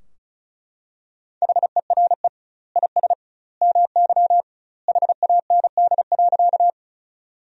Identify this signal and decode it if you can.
Morse code, heard as a keyed tone, about 36 wpm, 700 Hz. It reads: HERE IS MY HAND.
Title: 2x Word-Spacing - Sentences from Top 200 Words - 35wpm